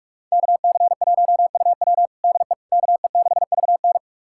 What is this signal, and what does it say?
Morse code, heard as a keyed tone, about 46 wpm, 700 Hz. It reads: KC1UW DE KE6VN
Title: USA Callsign DE Callsign - 5 Characters - 45wpm